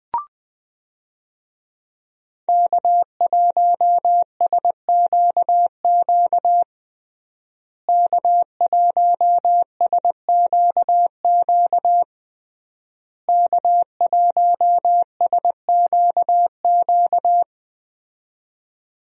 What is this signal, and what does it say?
Morse code, heard as a keyed tone, about 20 wpm, 700 Hz. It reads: K1SQQ K1SQQ K1SQQ
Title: USA Call Signs - 5 Characters - Repeated 3x - 20wpm